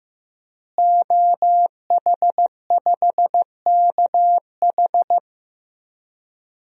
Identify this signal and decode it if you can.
Morse code, heard as a keyed tone, about 15 wpm, 700 Hz. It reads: OH5KH
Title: DXCC-100 Call Signs - 5 Characters - Repeated 2x - 15wpm